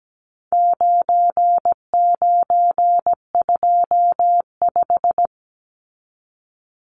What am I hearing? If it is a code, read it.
Morse code, heard as a keyed tone, about 17 wpm, 700 Hz. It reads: 9925
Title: Numbers - 4-Digit 17wpm